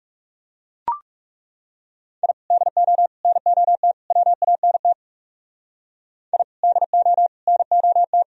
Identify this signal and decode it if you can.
Morse code, heard as a keyed tone, about 38 wpm, 700 Hz. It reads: I DO NOT WANT I DO NOT